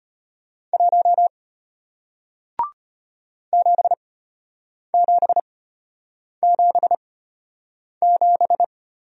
Morse code, heard as a keyed tone, about 33 wpm, spaced as for 8 wpm, 700 Hz. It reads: 1 7777